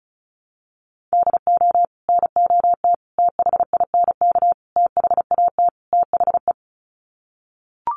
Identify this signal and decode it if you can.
Morse code, heard as a keyed tone, about 34 wpm, 700 Hz. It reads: DO NOT THINK THAT THE
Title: Sets of 5 Words Encore2 35wpm